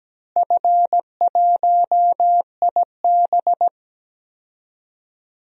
Morse code, heard as a keyed tone, about 17 wpm, 700 Hz. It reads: F1IB
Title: DXCC-100 Call Signs - 4 Characters - Repeated 3x - 17wpm